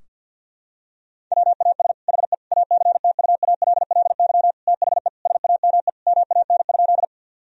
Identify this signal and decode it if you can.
Morse code, heard as a keyed tone, about 50 wpm, 700 Hz. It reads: WAS HE ACTUALLY THE SAME MAN?
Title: Sentences from Top 1000 Words - Review Part 4 - 50wpm